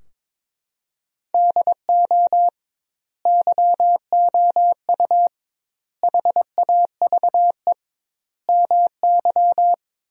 Morse code, heard as a keyed tone, about 21 wpm, 700 Hz. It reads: DO YOU HAVE MY
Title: 2x Word-Spacing - Sentences from Top 500 Words - 22wpm